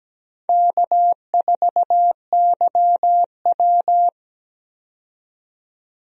Morse code, heard as a keyed tone, about 17 wpm, 700 Hz. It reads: K4YW
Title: USA Call Signs - 4 Characters - 17wpm